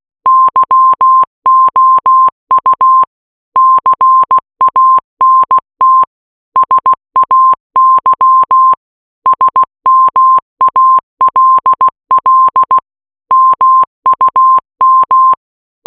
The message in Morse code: YOU CANT SAY SMALL MUM